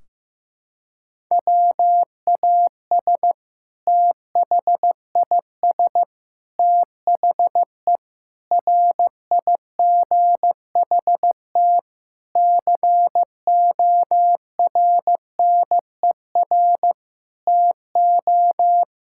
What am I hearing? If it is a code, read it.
Morse code, heard as a keyed tone, about 15 wpm, 700 Hz. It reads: WAS THIS THE RIGHT CORNER TO
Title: Sentences from Top 1600 Words 15wpm